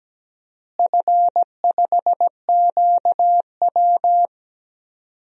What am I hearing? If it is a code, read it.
Morse code, heard as a keyed tone, about 17 wpm, 700 Hz. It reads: F5QW
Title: DXCC-100 Call Signs - 4 Characters - Repeated 3x - 17wpm